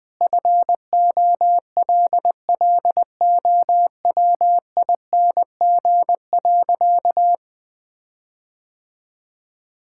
Morse code, heard as a keyed tone, about 20 wpm, 700 Hz. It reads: FOLLOWING.
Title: Sentences with 9-Word Phrases - 20wpm